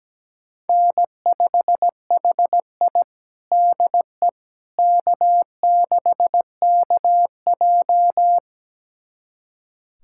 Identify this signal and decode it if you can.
Morse code, heard as a keyed tone, about 17 wpm, 700 Hz. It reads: N5HI DE K6KJ
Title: USA Callsign DE Callsign - 4 Characters - 17wpm